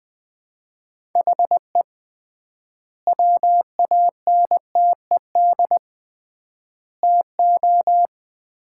Morse code, heard as a keyed tone, about 20 wpm, 700 Hz. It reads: HE WANTED TO
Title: 3x Word-Spacing - Sets of 3 Words - 20wpm